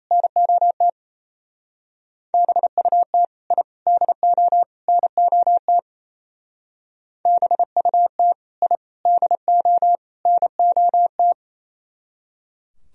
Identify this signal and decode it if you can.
Morse code, heard as a keyed tone, about 36 wpm, 700 Hz. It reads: NOT BUT I DO NOT BUT I DO NOT